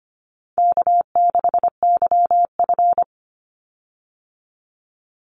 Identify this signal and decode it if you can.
Morse code, heard as a keyed tone, about 25 wpm, 700 Hz. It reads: K6YF
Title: Easy Call Signs Repeated 2x - 25wpm